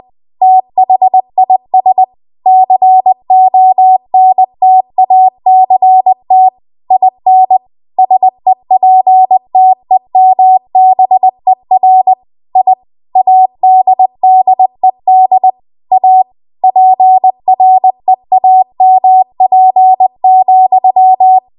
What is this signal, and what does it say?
Morse code, heard as a keyed tone, about 20 wpm, 750 Hz. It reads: THIS CONTACT IN SEPTEMBER I ADDED A PREAMP,
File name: Morse / 20WPM